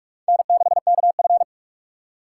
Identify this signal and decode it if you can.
Morse code, heard as a keyed tone, about 58 wpm, 700 Hz. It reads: N6KF